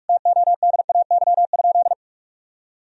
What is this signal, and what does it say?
Morse code, heard as a keyed tone, about 46 wpm, 700 Hz. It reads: TODAY?
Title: Sentences from Top 1000 Words - Review Part 1 - 45wpm